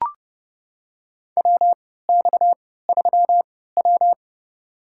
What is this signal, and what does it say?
Morse code, heard as a keyed tone, about 30 wpm, spaced as for 17 wpm, 700 Hz. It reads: WX3W
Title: USA Call Signs - 4 Characters - 3x Character Spacing - 30wpm